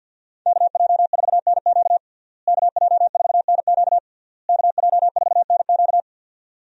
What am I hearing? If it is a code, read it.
Morse code, heard as a keyed tone, about 52 wpm, 700 Hz. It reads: KJ4NQ KJ4NQ KJ4NQ
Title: USA Call Signs - 5 Characters - Repeated 3x - 50wpm